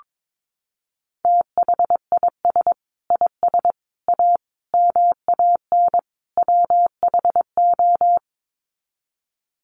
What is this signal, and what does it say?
Morse code, heard as a keyed tone, about 22 wpm, 700 Hz. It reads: THIS IS A MAN WHO
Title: Sets of 5 Words Encore2 22wpm